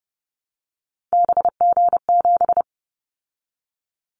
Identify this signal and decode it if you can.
Morse code, heard as a keyed tone, about 30 wpm, 700 Hz. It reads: BG7
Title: Callsign Prefixes for Top 100 DXCC - 30wpm